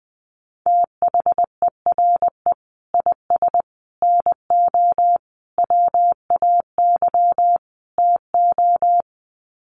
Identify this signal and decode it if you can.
Morse code, heard as a keyed tone, about 19 wpm, 700 Hz. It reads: THERE IS NO WAY TO
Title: Sets of 5 Words Encore 20wpm